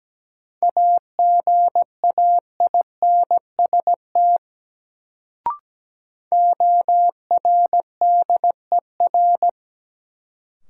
Morse code, heard as a keyed tone, about 17 wpm, 700 Hz. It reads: AGAINST ORDER
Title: Sentences from Top 400 Words - New Word Drill - 17wpm